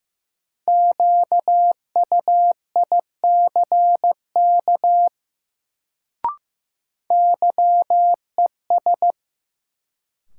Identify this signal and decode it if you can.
Morse code, heard as a keyed tone, about 15 wpm, 700 Hz. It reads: QUICK YES
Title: Sentences from Top 500 Words - New Word Drill - 15wpm